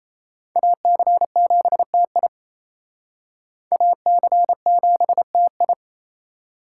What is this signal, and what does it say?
Morse code, heard as a keyed tone, about 34 wpm, 700 Hz. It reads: AC7TI AC7TI
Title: Speed Racing - USA Call Signs 28wpm